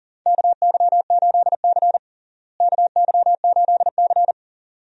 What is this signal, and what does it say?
Morse code, heard as a keyed tone, about 40 wpm, 700 Hz. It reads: KY8C KY8C
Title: USA Call Signs - 4 Characters - Repeated 3x - 40wpm